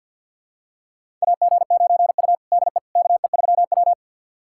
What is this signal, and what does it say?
Morse code, heard as a keyed tone, about 48 wpm, 700 Hz. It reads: AG9U DE KE3W
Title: USA Callsign DE Callsign - 4 Characters - 50wpm